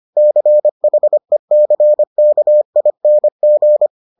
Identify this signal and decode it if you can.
Morse code, heard as a keyed tone, about 25 wpm, 600 Hz. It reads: CHECKING